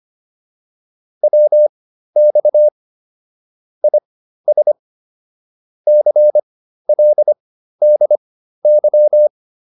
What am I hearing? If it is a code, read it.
Morse code, heard as a keyed tone, about 25 wpm, spaced as for 13 wpm, 600 Hz. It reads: WX IS CLDY